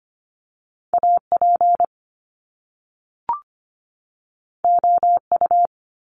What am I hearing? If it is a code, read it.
Morse code, heard as a keyed tone, about 25 wpm, 700 Hz. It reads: AP OU
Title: Top Two Letter Combinations in Words 25wpm